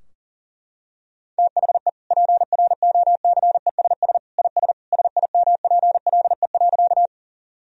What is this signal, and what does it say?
Morse code, heard as a keyed tone, about 40 wpm, 700 Hz. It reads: THE PROCESS IS SIMPLE.
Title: Sentences from Top 1000 Words - Review Part 3 - 40wpm